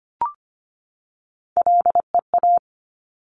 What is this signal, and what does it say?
Morse code, heard as a keyed tone, about 25 wpm, 700 Hz. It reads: LEA
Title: Top Three Letter Combinations in Words 25wpm